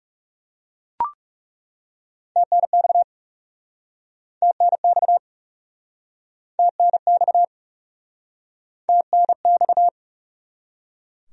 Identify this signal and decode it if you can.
Morse code, heard as a keyed tone, about 40 wpm, 700 Hz. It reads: TNX TNX TNX TNX